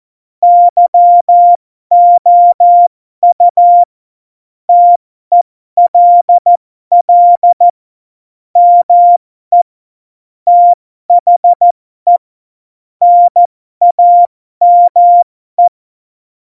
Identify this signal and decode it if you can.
Morse code, heard as a keyed tone, about 14 wpm, spaced as for 12 wpm, 700 Hz. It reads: YOU TELL ME THE NAME
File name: Morsekurs / Witze